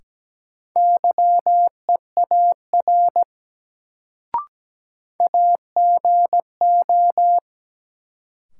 Morse code, heard as a keyed tone, about 17 wpm, 700 Hz. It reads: YEAR AGO
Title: Sets of 2 Words - New Word Drill - 17wpm